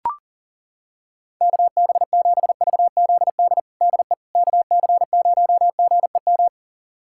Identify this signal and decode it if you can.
Morse code, heard as a keyed tone, about 37 wpm, 700 Hz. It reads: KB7VZD DE KC0GEM
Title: USA Callsign DE Callsign 40wpm